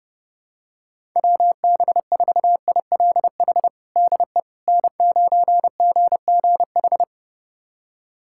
Morse code, heard as a keyed tone, about 31 wpm, 700 Hz. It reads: WB4ILH DE N9GGH
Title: USA Callsign DE Callsign 30wpm